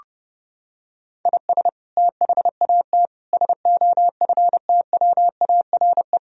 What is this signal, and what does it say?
Morse code, heard as a keyed tone, about 32 wpm, 700 Hz. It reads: IS THAT SOFTWARE
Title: Sentences from Top 1000 Words - Review Part 3 - 30wpm